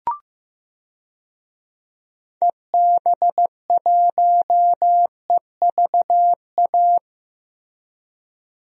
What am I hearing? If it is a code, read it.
Morse code, heard as a keyed tone, about 15 wpm, 700 Hz. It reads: EB1EVA
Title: DXCC-100 Call Signs - 6 Characters - Repeated 2x - 15wpm